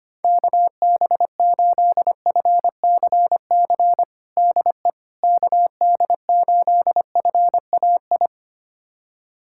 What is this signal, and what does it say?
Morse code, heard as a keyed tone, about 25 wpm, 700 Hz. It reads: KB8FCC DE KD8FAI
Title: USA Callsign DE Callsign 25wpm